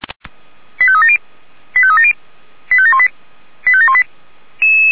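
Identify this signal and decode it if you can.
Five-tone selective call (ZVEI), heard as quick runs of five tones, tones of about 70 ms. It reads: 85370 85370 85718 85718
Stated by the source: ZVEI five-tone call received by SDR.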